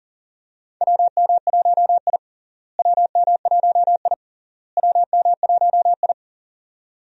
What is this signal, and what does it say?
Morse code, heard as a keyed tone, about 40 wpm, 700 Hz. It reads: WM1I WM1I WM1I
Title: USA Call Signs - 4 Characters - Repeated 3x - 40wpm